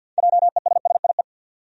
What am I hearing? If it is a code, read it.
Morse code, heard as a keyed tone, about 50 wpm, 700 Hz. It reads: JESSIE
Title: USA Common Names - 50wpm